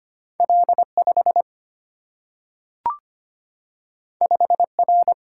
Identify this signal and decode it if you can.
Morse code, heard as a keyed tone, about 25 wpm, 700 Hz. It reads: L 5 5 R
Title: Single-Letter Single-Number 25wpm